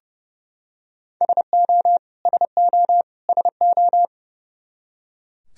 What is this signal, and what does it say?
Morse code, heard as a keyed tone, about 32 wpm, 700 Hz. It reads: SO SO SO
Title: Top 100 Words Repeated 30wpm